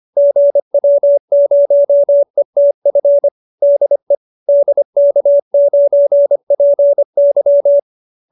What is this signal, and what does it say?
Morse code, heard as a keyed tone, about 25 wpm, 570 Hz. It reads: GW0ETF DE DK9PY